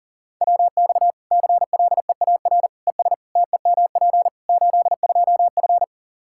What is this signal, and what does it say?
Morse code, heard as a keyed tone, about 40 wpm, 700 Hz. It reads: WX CLEAR ES TEMP 82F
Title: Practice QSO - Exchange 2 - Info - 40wpm